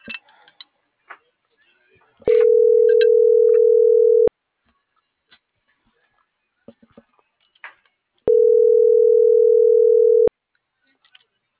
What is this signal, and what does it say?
Ringback tone (North America), 440 + 480 Hz.